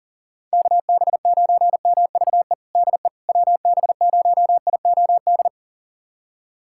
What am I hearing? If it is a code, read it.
Morse code, heard as a keyed tone, about 37 wpm, 700 Hz. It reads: KB9MVE DE WB0IOD
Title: USA Callsign DE Callsign 40wpm